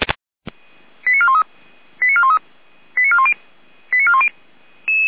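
Five-tone selective call (ZVEI), heard as quick runs of five tones, tones of about 80 ms. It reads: 89413 89413 89410 89410
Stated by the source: ZVEI five-tone selective call received by SDR.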